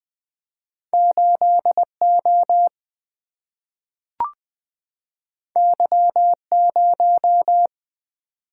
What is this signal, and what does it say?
Morse code, heard as a keyed tone, about 20 wpm, 700 Hz. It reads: 8O Y0
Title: Single-Letter Single-Number 20wpm